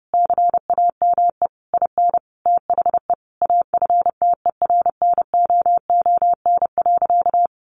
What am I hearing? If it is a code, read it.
Morse code, heard as a keyed tone, about 28 wpm, 700 Hz. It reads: CAME IN THE AFTERNOON.
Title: Sentences from Top 1500 Words 30wpm